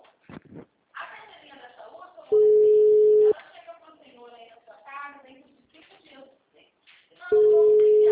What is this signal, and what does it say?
Ringback tone (Europe, 425 Hz), with speech.